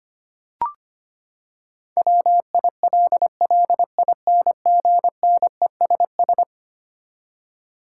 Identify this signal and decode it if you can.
Morse code, heard as a keyed tone, about 25 wpm, 700 Hz. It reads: WILLINGNESS
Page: morse-code-ninja-practice-25wpm